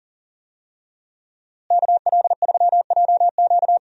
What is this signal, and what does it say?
Morse code, heard as a keyed tone, about 40 wpm, 700 Hz. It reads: KL3JQ
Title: USA Call Signs - 5 Characters - Repeated 2x - 40wpm